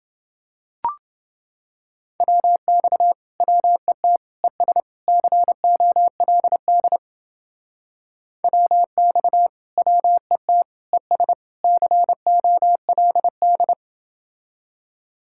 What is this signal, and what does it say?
Morse code, heard as a keyed tone, about 28 wpm, 700 Hz. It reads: WX WET ES COLD WX WET ES COLD